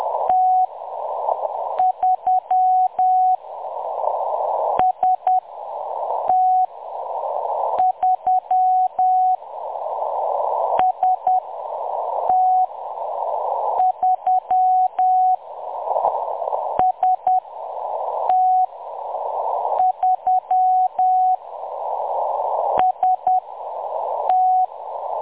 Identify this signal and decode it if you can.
Morse code, heard as a keyed tone, about 10 wpm, spaced as for 6 wpm, 740 Hz. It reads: T3 ST3 ST3 ST3 ST
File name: M23-ST3- -0858utc-10184khz